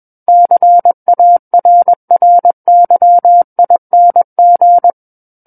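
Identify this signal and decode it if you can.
Morse code, heard as a keyed tone, about 21 wpm, 700 Hz. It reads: CARRYING
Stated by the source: Morse code practice reading of a book chapter at 21 words per minute.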